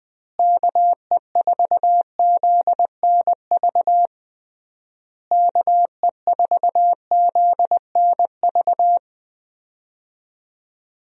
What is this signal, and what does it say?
Morse code, heard as a keyed tone, about 21 wpm, 700 Hz. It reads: KE4ZNV KE4ZNV
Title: USA Call Signs - 6 Characters - Repeated 3x - 20wpm